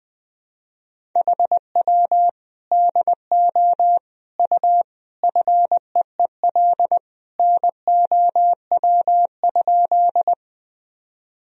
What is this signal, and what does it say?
Morse code, heard as a keyed tone, about 21 wpm, 700 Hz. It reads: HW DO U FEEL NOW?
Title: Sentences from Top 400 Words and Abbreviations 20wpm